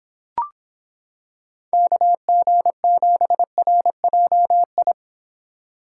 Morse code, heard as a keyed tone, about 26 wpm, 700 Hz. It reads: KG7RJI